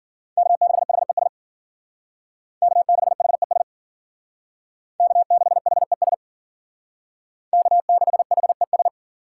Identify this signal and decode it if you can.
Morse code, heard as a keyed tone, about 51 wpm, 700 Hz. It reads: K6HES K6HES K6HES K6HES